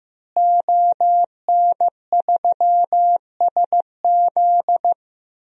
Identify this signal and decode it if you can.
Morse code, heard as a keyed tone, about 16 wpm, 700 Hz. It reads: ON3SZ
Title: DXCC-100 Call Signs - 5 Characters - Repeated 3x - 15wpm